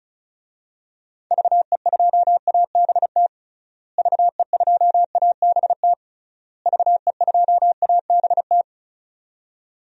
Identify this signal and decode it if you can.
Morse code, heard as a keyed tone, about 34 wpm, 700 Hz. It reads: VE2ABT VE2ABT VE2ABT
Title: DXCC-100 Call Signs - 6 Characters - Repeated 3x - 35wpm